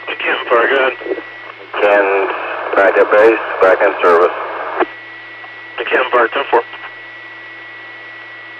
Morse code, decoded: none

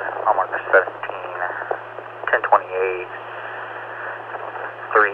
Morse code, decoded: none

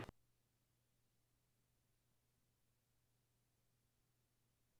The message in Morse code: none

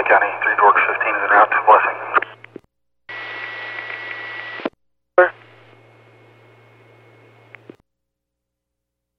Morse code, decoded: none

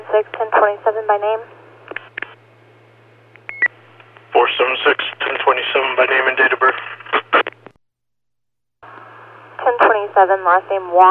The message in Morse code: none

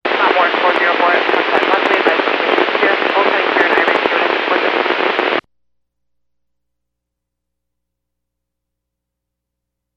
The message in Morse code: none

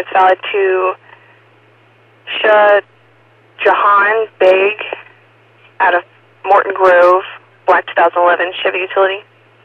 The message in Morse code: none

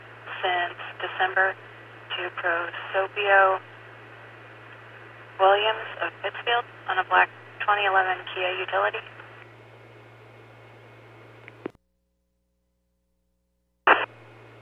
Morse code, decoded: none

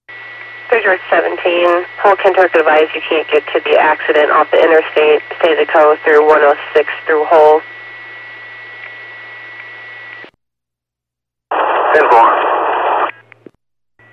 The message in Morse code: none

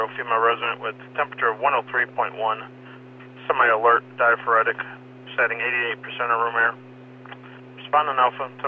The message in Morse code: none

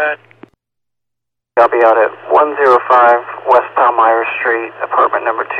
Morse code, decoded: none

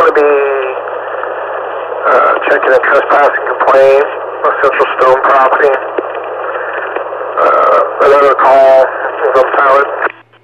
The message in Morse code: none

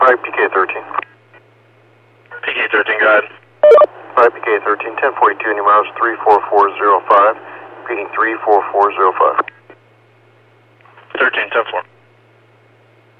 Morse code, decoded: none

